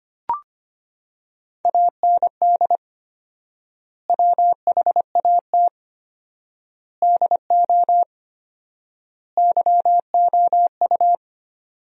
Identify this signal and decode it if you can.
Morse code, heard as a keyed tone, about 25 wpm, 700 Hz. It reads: AND WHAT DO YOU
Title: 4x Word-Spacing - Sentences from Top 100 Words - 25wpm